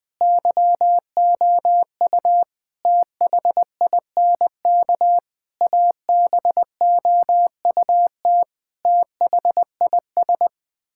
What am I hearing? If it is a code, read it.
Morse code, heard as a keyed tone, about 20 wpm, 700 Hz. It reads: YOU THINK ABOUT THIS
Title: Sets of 9 Words - 20wpm